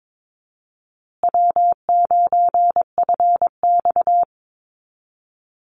Morse code, heard as a keyed tone, about 22 wpm, 700 Hz. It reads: W9FX
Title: Easy Call Signs Repeated 3x - 22wpm